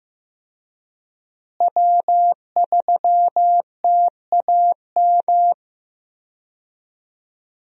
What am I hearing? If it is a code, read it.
Morse code, heard as a keyed tone, about 15 wpm, 700 Hz. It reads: W3TAM
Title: USA Call Signs - 5 Characters - Repeated 3x - 15wpm